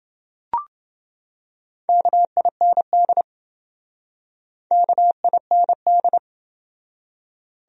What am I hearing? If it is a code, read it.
Morse code, heard as a keyed tone, about 27 wpm, 700 Hz. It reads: KIND KIND